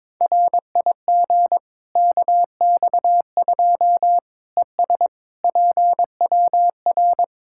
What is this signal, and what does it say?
Morse code, heard as a keyed tone, about 22 wpm, 700 Hz. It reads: RIG KX2 ES PWR
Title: K1USN SST QSO Practice 22wpm